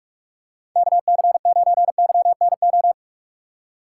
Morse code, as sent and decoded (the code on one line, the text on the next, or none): -.- -.-. ----. -.-- -. ---
KC9YNO